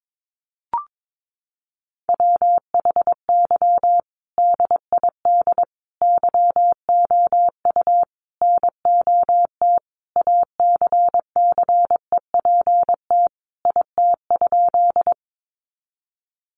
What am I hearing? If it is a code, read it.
Morse code, heard as a keyed tone, about 22 wpm, 700 Hz. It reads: WHY DID YOU NOT ACCEPT IT?
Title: Sentences from Top 1300 Words 22wpm